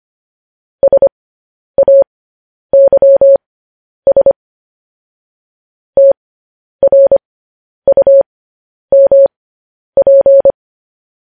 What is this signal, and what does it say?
Morse code, heard as a keyed tone, about 25 wpm, spaced as for 10 wpm, 550 Hz. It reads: SAYS TRUMP